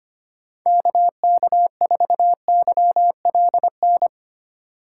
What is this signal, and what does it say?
Morse code, heard as a keyed tone, about 25 wpm, 700 Hz. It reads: KK4YLN
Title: USA Call Signs - 6 Characters - 2x Character Spacing - 25wpm